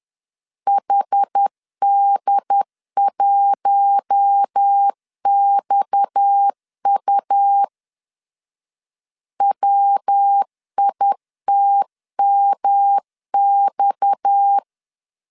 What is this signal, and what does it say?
Morse code, heard as a keyed tone, about 10 wpm, 800 Hz. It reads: HD1XU WITMX
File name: LEZIONI / 6 PKX0 / 6 PKX 10wpm 2426